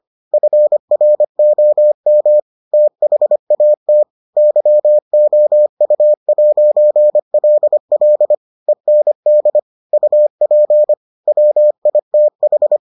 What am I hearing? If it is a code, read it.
Morse code, heard as a keyed tone, about 25 wpm, 600 Hz. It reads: FROM THAT YOU'LL END UP WITH